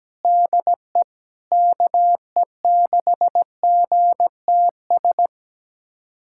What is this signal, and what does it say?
Morse code, heard as a keyed tone, about 17 wpm, 700 Hz. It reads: DE KE6GTS
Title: USA Callsign DE Callsign - 6 Characters - 17wpm